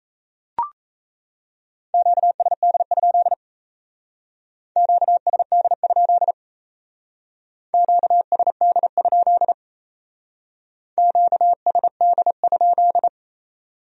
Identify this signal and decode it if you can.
Morse code, heard as a keyed tone, about 33 wpm, 700 Hz. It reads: QSD? QSD? QSD? QSD?
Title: Speed-Racing Q-Codes - 28wpm